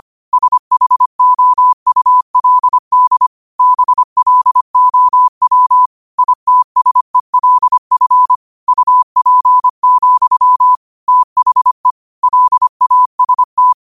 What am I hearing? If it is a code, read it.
Morse code, heard as a keyed tone, about 25 wpm, 1000 Hz. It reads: SHOULD BLOW ITSELF UP, THE LAST